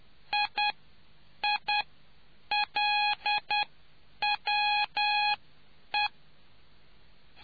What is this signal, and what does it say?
Morse code, heard as a keyed tone, about 10 wpm, spaced as for 8 wpm, 830 Hz. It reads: IILWE